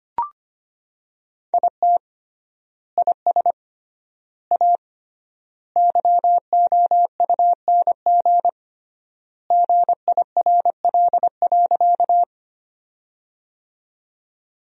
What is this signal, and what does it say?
Morse code, heard as a keyed tone, about 25 wpm, 700 Hz. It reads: IT IS A YOUNG GIRL.
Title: 3x Word-Spacing - Sentences from Top 400 Words - 25wpm